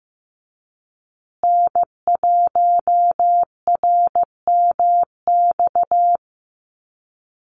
Morse code, heard as a keyed tone, about 15 wpm, 700 Hz. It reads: N1RMX
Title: USA Call Signs Repeated 2x 15wpm